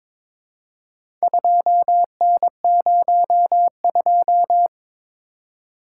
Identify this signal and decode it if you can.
Morse code, heard as a keyed tone, about 23 wpm, 700 Hz. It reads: 2N02
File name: practice / 22wpm / Four Letter-Number Combinations - 22wpm